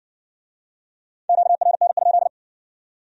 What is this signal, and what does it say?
Morse code, heard as a keyed tone, about 60 wpm, 700 Hz. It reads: QRN?